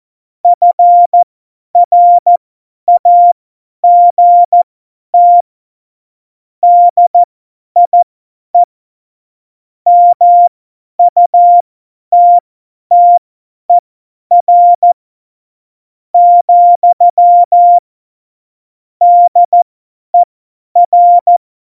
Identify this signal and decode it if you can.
Morse code, heard as a keyed tone, about 14 wpm, spaced as for 10 wpm, 700 Hz. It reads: FRAGT DIE MUTTER , DER
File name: Morsekurs / Witze / Witz 01. Mai 2018, 50 F70